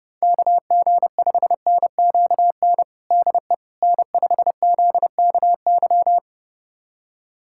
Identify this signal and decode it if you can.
Morse code, heard as a keyed tone, about 30 wpm, 690 Hz. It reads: KG5NQN DE N5ZKY
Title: USA Callsign DE Callsign 30wpm